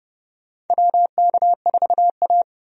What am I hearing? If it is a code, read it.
Morse code, heard as a keyed tone, about 29 wpm, 700 Hz. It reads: WK4A
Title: USA Call Signs - 4 Characters - 30wpm